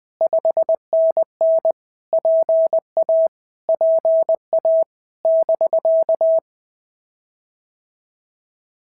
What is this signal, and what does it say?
Morse code, heard as a keyed tone, about 20 wpm, 650 Hz. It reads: 5NN PA PA <BK>
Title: POTA QSO Practice - 20wpm